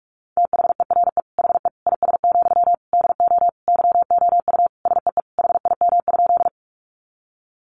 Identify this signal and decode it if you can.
Morse code, heard as a keyed tone, about 44 wpm, 700 Hz. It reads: THERE HE IS, DO YOU SEE HIM?
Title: Sentences from Top 100 Words 45wpm